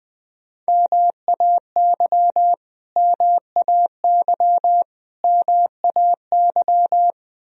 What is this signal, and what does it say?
Morse code, heard as a keyed tone, about 20 wpm, 700 Hz. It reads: MAY MAY MAY
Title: Top 100 Words Repeated 20wpm